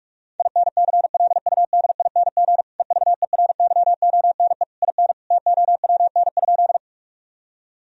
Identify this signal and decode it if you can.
Morse code, heard as a keyed tone, about 46 wpm, 700 Hz. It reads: INCLUDING EVERYONE IN TOWN?
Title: Sentences from Top 1000 Words - Review Part 4 - 45wpm